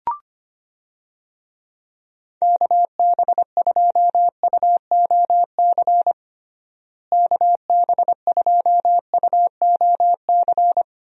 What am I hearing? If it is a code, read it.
Morse code, heard as a keyed tone, about 25 wpm, 700 Hz. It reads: KB2UOC KB2UOC
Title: USA Call Signs - 6 Characters - Repeated 2x - 25wpm